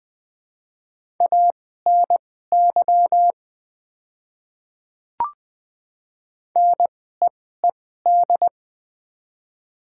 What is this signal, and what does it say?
Morse code, heard as a keyed tone, about 20 wpm, spaced as for 14 wpm, 700 Hz. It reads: ANY NEED
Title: 2x Character Spacing - Top 300 Words - 20wpm